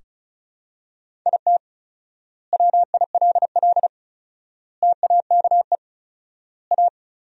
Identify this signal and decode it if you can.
Morse code, heard as a keyed tone, about 36 wpm, 700 Hz. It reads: IT WILL TAKE A
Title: 4x Word-Spacing - Sentences from Top 200 Words - 35wpm